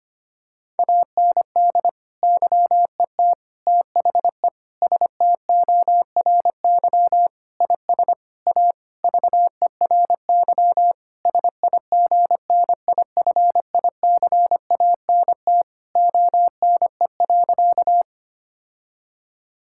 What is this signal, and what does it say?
Morse code, heard as a keyed tone, about 25 wpm, 700 Hz. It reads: AND YET THE STORY IS A VERY SIGNIFICANT ONE.
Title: Sentences from Top 1000 Words - Review Part 6 - 25wpm